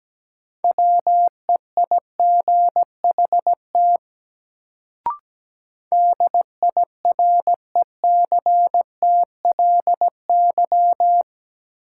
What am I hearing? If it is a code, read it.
Morse code, heard as a keyed tone, about 17 wpm, 700 Hz. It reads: WEIGHT DIRECTLY
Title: Sentences from Top 800 Words - New Word Drill - 17wpm